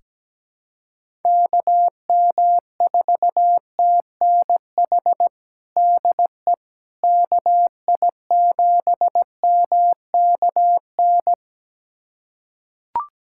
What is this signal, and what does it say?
Morse code, heard as a keyed tone, about 17 wpm, 700 Hz. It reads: KM4TNH DE KI7MKN
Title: USA Callsign DE Callsign - 6 Characters - 17wpm